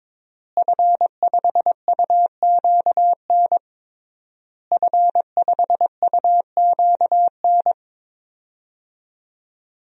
Morse code, heard as a keyed tone, about 22 wpm, 700 Hz. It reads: F5UQN F5UQN
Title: DXCC-100 Call Signs - 5 Characters - Repeated 3x - 22wpm